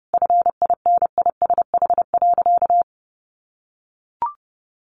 Morse code, heard as a keyed tone, about 30 wpm, 700 Hz. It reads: FINISH.
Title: Sentences from Top 1200 Words 30wpm